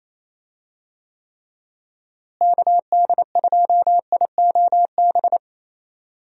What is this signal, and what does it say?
Morse code, heard as a keyed tone, about 28 wpm, 700 Hz. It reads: KD2IOB
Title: USA Call Signs - 6 Characters - Repeated 3x - 28wpm